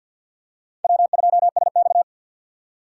Morse code, heard as a keyed tone, about 53 wpm, 700 Hz. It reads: W2SX
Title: USA Call Signs - 3x Character Spacing - 50wpm